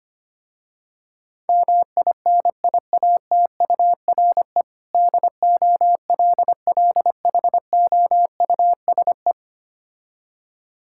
Morse code, heard as a keyed tone, about 25 wpm, 700 Hz. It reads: MINIATURE DOLLHOUSE
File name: practice / 25wpm / Difficult Words with Hints - 25wpm